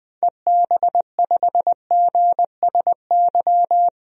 Morse code, heard as a keyed tone, about 20 wpm, 700 Hz. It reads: EB5GSY
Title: DXCC-100 Call Signs - 6 Characters - 20wpm